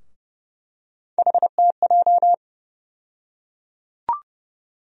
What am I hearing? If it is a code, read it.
Morse code, heard as a keyed tone, about 30 wpm, 700 Hz. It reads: HTJ
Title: USA Call Sign Postfixes - 3 Letters 30wpm